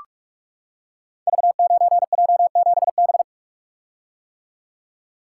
Morse code, heard as a keyed tone, about 44 wpm, 700 Hz. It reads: V9J7B
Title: Code Groups - 5 Letter:Numbers - 45wpm